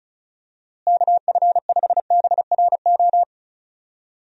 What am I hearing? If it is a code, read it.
Morse code, heard as a keyed tone, about 34 wpm, 700 Hz. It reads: KF5BRO
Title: USA Call Signs - 6 Characters - Repeated 3x - 35wpm